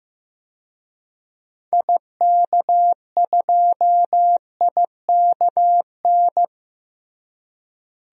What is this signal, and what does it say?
Morse code, heard as a keyed tone, about 16 wpm, 700 Hz. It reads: IK2IKN
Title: DXCC-100 Call Signs - 6 Characters - Repeated 3x - 15wpm